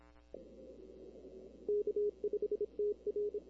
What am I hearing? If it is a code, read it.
Morse code, heard as a keyed tone, about 26 wpm, 420 Hz. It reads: K5TR
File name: CW / 2005-k5tr arrl cw dx -16:00